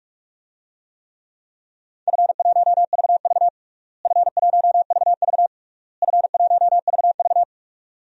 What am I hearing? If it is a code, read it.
Morse code, heard as a keyed tone, about 46 wpm, 700 Hz. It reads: F1VV F1VV F1VV
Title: DXCC-100 Call Signs - 4 Characters - Repeated 3x - 45wpm